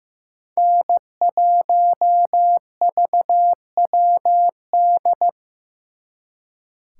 Morse code, heard as a keyed tone, about 18 wpm, 700 Hz. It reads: N1VWD
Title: Speed Racing - USA Call Signs 15wpm